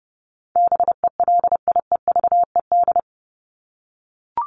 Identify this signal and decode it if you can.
Morse code, heard as a keyed tone, about 30 wpm, 700 Hz. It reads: BELIEVED
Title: ED Suffix - Top 100 Words with Base Words 30wpm